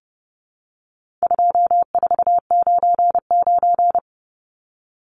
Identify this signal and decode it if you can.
Morse code, heard as a keyed tone, about 30 wpm, 700 Hz. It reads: 2499
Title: Numbers - 4-Digit 30wpm